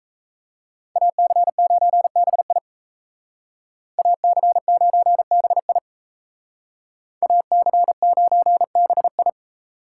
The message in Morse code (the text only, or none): AC9BI AC9BI AC9BI